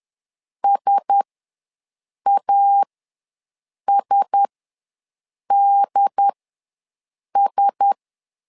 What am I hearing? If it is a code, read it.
Morse code, heard as a keyed tone, about 10 wpm, spaced as for 6 wpm, 800 Hz. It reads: SASDS
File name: LEZIONI / 4 Q1S ws 10wpm 2208